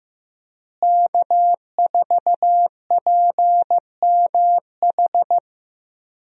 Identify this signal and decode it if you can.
Morse code, heard as a keyed tone, about 15 wpm, 700 Hz. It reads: K4PMH
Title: USA Call Signs - 5 Characters - Repeated 3x - 15wpm